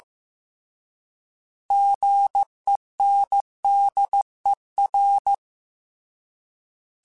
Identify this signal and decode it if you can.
Morse code, heard as a keyed tone, about 15 wpm, 800 Hz. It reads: GENDER